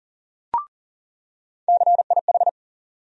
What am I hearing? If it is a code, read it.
Morse code, heard as a keyed tone, about 40 wpm, 700 Hz. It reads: CIH
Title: USA Call Sign Postfixes - 3 Letters 40wpm